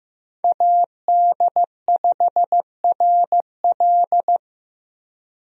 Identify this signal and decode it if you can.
Morse code, heard as a keyed tone, about 15 wpm, 700 Hz. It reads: AD5RL
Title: USA Call Signs - 5 Characters - Repeated 3x - 15wpm